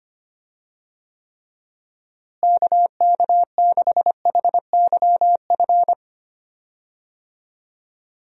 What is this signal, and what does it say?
Morse code, heard as a keyed tone, about 25 wpm, 700 Hz. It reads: KK6HYF
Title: USA Call Signs - 6 Characters - Repeated 3x - 25wpm